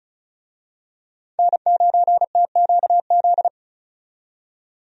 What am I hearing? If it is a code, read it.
Morse code, heard as a keyed tone, about 34 wpm, 700 Hz. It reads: N9TQZ
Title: USA Call Signs - 5 Characters - 35wpm